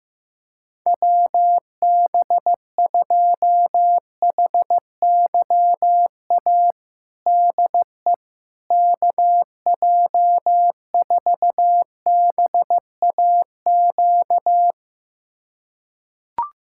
Morse code, heard as a keyed tone, about 15 wpm, 700 Hz. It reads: WB2HYA DE KJ4BAQ
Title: USA Callsign DE Callsign - 6 Characters - 15wpm